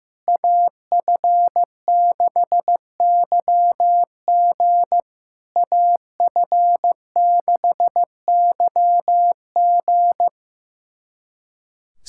Morse code, heard as a keyed tone, about 15 wpm, 700 Hz. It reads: AF6YG AF6YG
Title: SOTA QSO Practice 15wpm